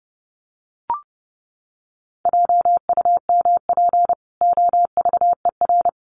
Morse code, heard as a keyed tone, about 28 wpm, 700 Hz. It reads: JUMP OVER
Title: Sentences from Top 1300 Words 30wpm